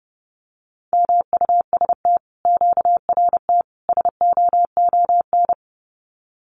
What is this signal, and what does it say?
Morse code, heard as a keyed tone, about 30 wpm, 700 Hz. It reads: MUST QRT SOON
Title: Common QSO Phrases Encore 30wpm